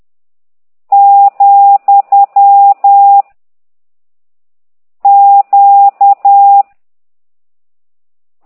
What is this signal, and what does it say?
Morse code, heard as a keyed tone, about 10 wpm, 800 Hz. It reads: , Q
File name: ct2idw / exerc15